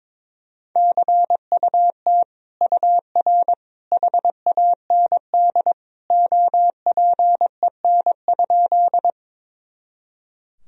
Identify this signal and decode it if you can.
Morse code, heard as a keyed tone, about 22 wpm, 700 Hz. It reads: CUT UR HAND OPEN?
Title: Sentences from Top 300 Words and Abbreviations 22wpm